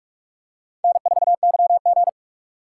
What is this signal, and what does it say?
Morse code, heard as a keyed tone, about 46 wpm, 700 Hz. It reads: N4YG
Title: USA Call Signs - 4 Characters - 4x Character Spacing - 45wpm